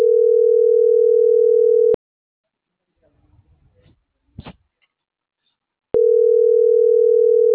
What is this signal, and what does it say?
Ringback tone (North America), 440 + 480 Hz.
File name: CENTRAL MAERSK / out gabrielle.melendez - .5863